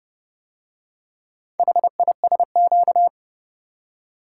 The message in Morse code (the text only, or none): HISQ